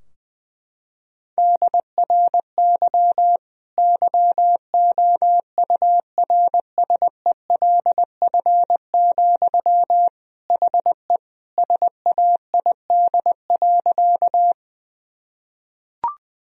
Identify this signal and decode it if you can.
Morse code, heard as a keyed tone, about 20 wpm, 700 Hz. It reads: DRY YOURSELF, HE SAID.
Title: Sentences from Top 1000 Words - Review Part 3 - 20wpm